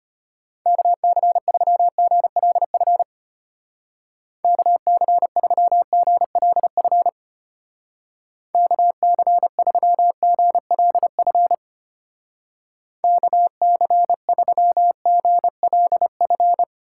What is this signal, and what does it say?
Morse code, heard as a keyed tone, about 32 wpm, 700 Hz. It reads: KC3GLF KC3GLF KC3GLF KC3GLF